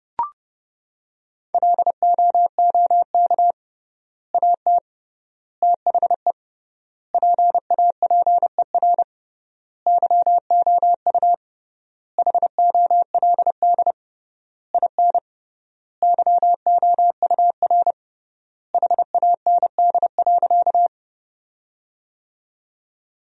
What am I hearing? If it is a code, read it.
Morse code, heard as a keyed tone, about 30 wpm, 700 Hz. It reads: LOOK AT THE PAPER YOU HOLD IN YOUR HAND.
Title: 3x Word-Spacing - Sentences from Top 400 Words - 30wpm